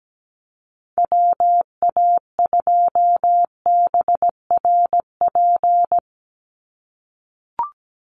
Morse code, heard as a keyed tone, about 17 wpm, 700 Hz. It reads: WA2BRP
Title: USA Call Signs 17wpm